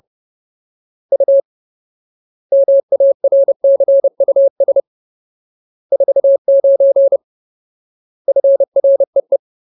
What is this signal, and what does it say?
Morse code, heard as a keyed tone, about 30 wpm, 550 Hz. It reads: U MARCUS 49 FREE
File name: Ultimate Word List / 30WPM Ultimate